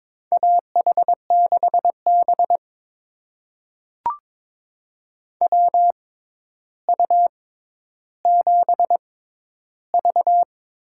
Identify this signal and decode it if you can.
Morse code, heard as a keyed tone, about 22 wpm, spaced as for 11 wpm, 700 Hz. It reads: AH6B WU7V
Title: USA Call Signs - 4 Characters - 6x Character Spacing - 22wpm